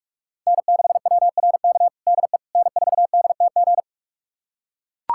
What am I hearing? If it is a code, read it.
Morse code, heard as a keyed tone, about 41 wpm, 690 Hz. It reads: N6WRK DE N4DTG
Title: USA Callsign DE Callsign 45wpm